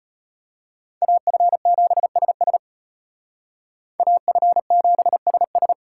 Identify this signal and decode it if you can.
Morse code, heard as a keyed tone, about 32 wpm, 700 Hz. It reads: AF7SS AF7SS